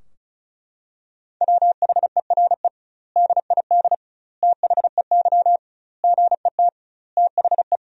Morse code, heard as a keyed tone, about 36 wpm, 700 Hz. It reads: WHERE DID THEY GET THE
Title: 2x Word-Spacing - Sentences from Top 400 Words - 35wpm